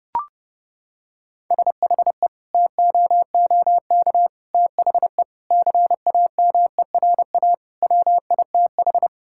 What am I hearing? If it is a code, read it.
Morse code, heard as a keyed tone, about 30 wpm, 700 Hz. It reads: SHE TOOK THE CAMERA WITH
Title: Sentences from Top 1000 Words - Review Part 5 - 30wpm